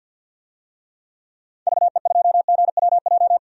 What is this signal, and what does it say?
Morse code, heard as a keyed tone, about 53 wpm, 700 Hz. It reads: VE2GWJ
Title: DXCC-100 Call Signs - 6 Characters - 50wpm